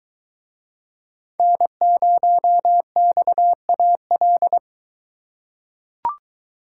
Morse code, heard as a keyed tone, about 22 wpm, 700 Hz. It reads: N0XAL